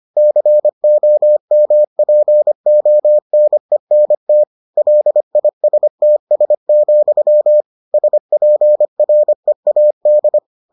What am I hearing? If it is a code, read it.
Morse code, heard as a keyed tone, about 25 wpm, 600 Hz. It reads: COMPONENT LISTS, SPREAD